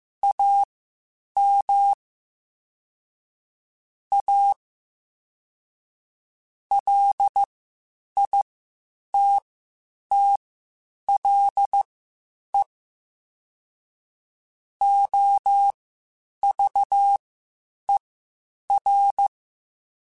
Morse code, heard as a keyed tone, about 15 wpm, spaced as for 8 wpm, 800 Hz. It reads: AM A LITTLE OVER
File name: (spaced) / QSOs Spaced-mp3s / WA7QOX 2 @13 sp